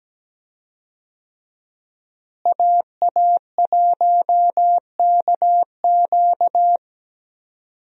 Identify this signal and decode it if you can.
Morse code, heard as a keyed tone, about 17 wpm, 700 Hz. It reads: AA1KQ
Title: USA Call Signs - 5 Characters - Repeated 3x - 17wpm